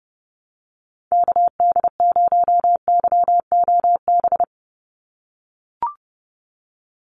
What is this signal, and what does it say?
Morse code, heard as a keyed tone, about 30 wpm, 700 Hz. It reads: KD0YOB
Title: NA SOTA Activator Callsigns 30wpm